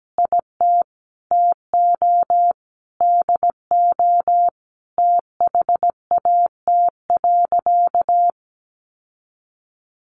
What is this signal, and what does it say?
Morse code, heard as a keyed tone, about 17 wpm, 700 Hz. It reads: IT TO DO THAT.
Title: Sentences from Top 1500 Words 17wpm